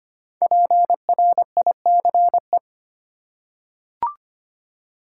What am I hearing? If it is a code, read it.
Morse code, heard as a keyed tone, about 25 wpm, 700 Hz. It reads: PRICE
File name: 5 Letter Words - First 3 Letters Given - 25wpm